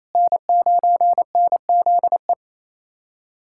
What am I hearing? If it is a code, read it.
Morse code, heard as a keyed tone, about 27 wpm, 700 Hz. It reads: N9NZE
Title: USA Callsign DE Callsign - 5 Characters - 28wpm